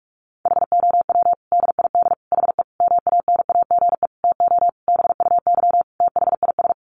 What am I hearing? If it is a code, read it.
Morse code, heard as a keyed tone, about 44 wpm, 700 Hz. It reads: HOW DID HE MANAGE TO BUY THIS
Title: Sentences from Top 1400 Words 45wpm